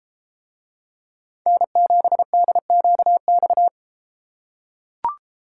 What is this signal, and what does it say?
Morse code, heard as a keyed tone, about 32 wpm, 700 Hz. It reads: N7DQX